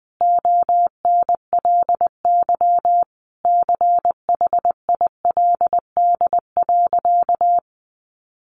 Morse code, heard as a keyed tone, about 19 wpm, 700 Hz. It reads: ONLY CHILD.
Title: Sentences from Top 800 Words 20wpm